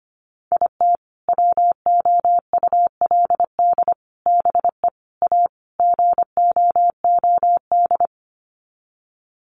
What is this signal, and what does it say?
Morse code, heard as a keyed tone, about 25 wpm, 700 Hz. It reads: IT WOULD BE A GOOD
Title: Sets of 5 Words Encore2 25wpm